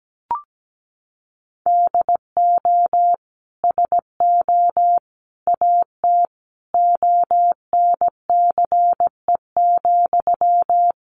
Morse code, heard as a keyed tone, about 17 wpm, 700 Hz. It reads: DO SO AT ONCE,
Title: Sentences from Top 1500 Words 17wpm